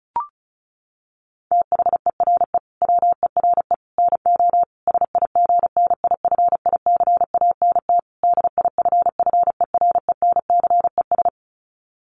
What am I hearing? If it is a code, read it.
Morse code, heard as a keyed tone, about 34 wpm, 700 Hz. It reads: THERE WERE NO SIGNIFICANT DIFFERENCES
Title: Sets of 5 Words Encore2 35wpm